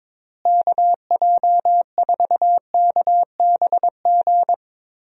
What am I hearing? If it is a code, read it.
Morse code, heard as a keyed tone, about 22 wpm, 700 Hz. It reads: KJ4KBG
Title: USA Call Signs - 6 Characters - Repeated 2x - 22wpm